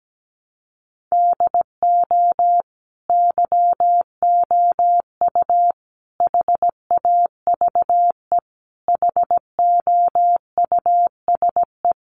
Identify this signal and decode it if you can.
Morse code, heard as a keyed tone, about 17 wpm, 700 Hz. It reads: DO YOU HAVE HOUSE
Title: Sentences from Top 1100 Words 17wpm